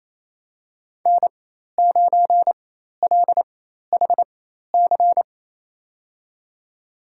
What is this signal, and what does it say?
Morse code, heard as a keyed tone, about 27 wpm, spaced as for 13 wpm, 700 Hz. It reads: N9LHC
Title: USA Call Signs - 4x Character Spacing - 28wpm